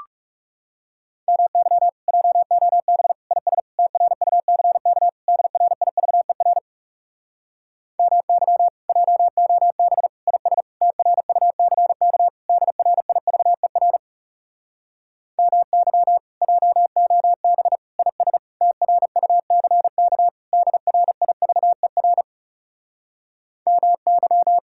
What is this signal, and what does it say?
Morse code, heard as a keyed tone, about 40 wpm, 700 Hz. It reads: MY JOB IS TRUCK DRIVER MY JOB IS TRUCK DRIVER MY JOB IS TRUCK DRIVER MY